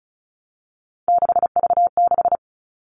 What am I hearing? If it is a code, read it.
Morse code, heard as a keyed tone, about 34 wpm, 700 Hz. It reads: 6V6
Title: All Cut Numbers - 3-Digit 35wpm